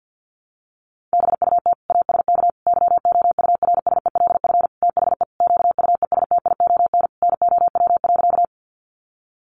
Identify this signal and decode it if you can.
Morse code, heard as a keyed tone, about 50 wpm, 700 Hz. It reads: BUT ASK YOURSELF THE QUESTION NOW.
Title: Sentences from Top 600 Words 50wpm